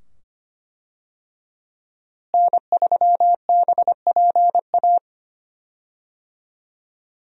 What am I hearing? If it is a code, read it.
Morse code, heard as a keyed tone, about 25 wpm, 700 Hz. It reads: N3BPA
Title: USA Call Signs - 5 Characters - Repeated 2x - 25wpm